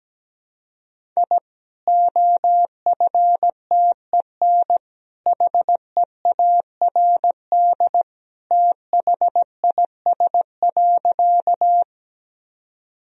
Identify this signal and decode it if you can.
Morse code, heard as a keyed tone, about 17 wpm, 700 Hz. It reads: I OFTEN HEARD THIS.
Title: Sentences from Top 1000 Words - Review Part 1 - 17wpm